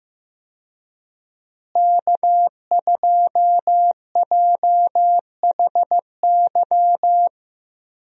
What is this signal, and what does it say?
Morse code, heard as a keyed tone, about 15 wpm, 700 Hz. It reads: K2JHY
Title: USA Call Signs - 5 Characters - Repeated 2x - 15wpm